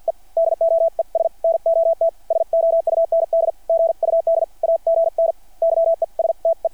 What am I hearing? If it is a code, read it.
Morse code, heard as a keyed tone, about 50 wpm, spaced as for 44 wpm, 660 Hz. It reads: E DOES NOT SOUND GUD AGN YESTE
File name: archived / CRSnet